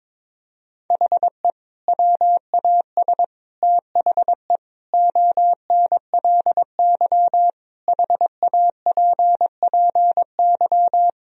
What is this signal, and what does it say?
Morse code, heard as a keyed tone, about 22 wpm, 700 Hz. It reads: HE WAS THE ONLY HAPPY
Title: Sentences from Top 1000 Words - Review Part 3 - 22wpm